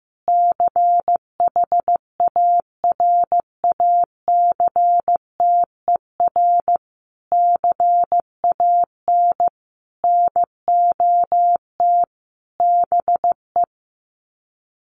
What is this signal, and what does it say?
Morse code, heard as a keyed tone, about 15 wpm, 700 Hz. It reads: CHARACTER CAN NOT BE
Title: Sets of 5 Words Encore2 15wpm